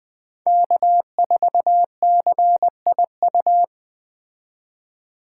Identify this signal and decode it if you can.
Morse code, heard as a keyed tone, about 20 wpm, 700 Hz. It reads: K4CIU